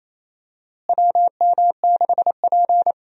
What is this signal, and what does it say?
Morse code, heard as a keyed tone, about 28 wpm, 700 Hz. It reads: WM6P
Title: USA Call Signs - 4 Characters - Repeated 3x - 28wpm